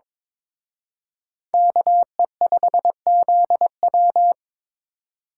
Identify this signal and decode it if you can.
Morse code, heard as a keyed tone, about 26 wpm, 700 Hz. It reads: KE5ZW
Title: Speed Racing - USA 2x2 Call Signs 22wpm